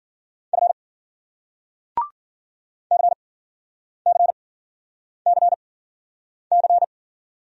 Morse code, heard as a keyed tone, about 52 wpm, spaced as for 9 wpm, 700 Hz. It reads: F CCCC